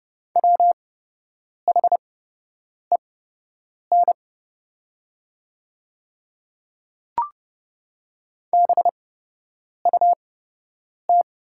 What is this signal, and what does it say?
Morse code, heard as a keyed tone, about 30 wpm, spaced as for 8 wpm, 700 Hz. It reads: WHEN BUT
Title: 8x Character Spacing - Top 100 Words - 30wpm